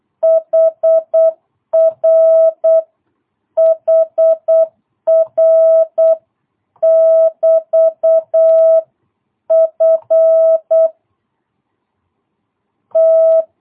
Morse code, heard as a keyed tone, about 8 wpm, 660 Hz. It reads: HR HR <BT> F T
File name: jh8cbh / 2022cw